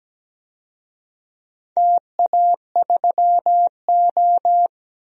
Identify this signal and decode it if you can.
Morse code, heard as a keyed tone, about 17 wpm, 700 Hz. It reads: TA3O
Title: DXCC-100 Call Signs - 4 Characters - 17wpm